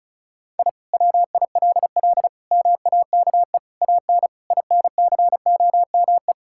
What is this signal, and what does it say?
Morse code, heard as a keyed tone, about 34 wpm, 700 Hz. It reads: I WILL MAKE AN INCOME
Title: Sentences from Top 1000 Words - Review Part 5 - 35wpm